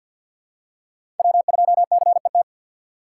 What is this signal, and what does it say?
Morse code, heard as a keyed tone, about 47 wpm, 700 Hz. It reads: W2CET
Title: USA Call Signs - 5 Characters - 6x Character Spacing - 50wpm